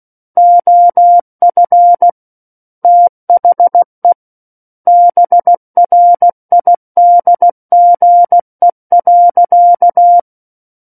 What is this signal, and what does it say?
Morse code, heard as a keyed tone, about 16 wpm, 700 Hz. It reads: OF THE BRIDGE.
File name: Book One / 07-Chapter 7 - 16 WPM